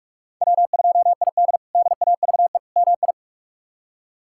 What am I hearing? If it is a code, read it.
Morse code, heard as a keyed tone, about 46 wpm, 700 Hz. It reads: W2ID DAVE MI
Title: K1USN SST QSO Practice - 45wpm